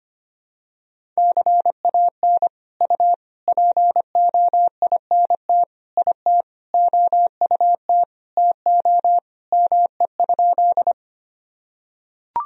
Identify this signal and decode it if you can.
Morse code, heard as a keyed tone, about 25 wpm, 700 Hz. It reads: CAN U POINT IT OUT TO ME?
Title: Sentences from Top 200 Words and Abbreviations 25wpm